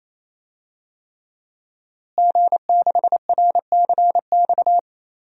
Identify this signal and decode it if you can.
Morse code, heard as a keyed tone, about 28 wpm, 700 Hz. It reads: G6RCX
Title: DXCC-100 Call Signs - 5 Characters - Repeated 3x - 28wpm